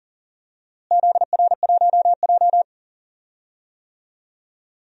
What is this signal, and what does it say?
Morse code, heard as a keyed tone, about 40 wpm, 700 Hz. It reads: ZR1J
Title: DXCC-100 Call Signs - 4 Characters - Repeated 2x - 40wpm